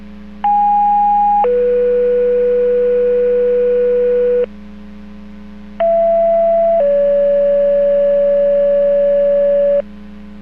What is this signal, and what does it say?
Two-tone page, heard as a one-second tone, then a three-second tone, about 820 then 490 Hz, about 670 then 570 Hz. It is heard from 0.4 to 9.8 s.